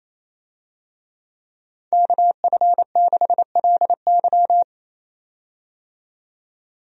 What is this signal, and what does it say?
Morse code, heard as a keyed tone, about 28 wpm, 700 Hz. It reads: KF6LY
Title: USA Call Signs - 5 Characters - 28wpm